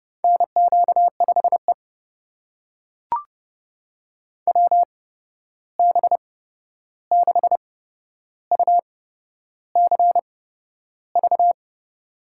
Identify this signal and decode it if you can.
Morse code, heard as a keyed tone, about 31 wpm, spaced as for 11 wpm, 700 Hz. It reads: NQ5E WB6UCV
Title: USA Call Signs - 8x Character Spacing - 30wpm